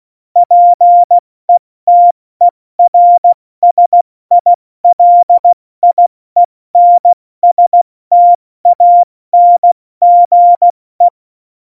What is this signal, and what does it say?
Morse code, heard as a keyed tone, about 16 wpm, 700 Hz. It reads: PETERSILIENSTANGE
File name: Morsekurs / Witze / Witz 16. April 2019, 70 F80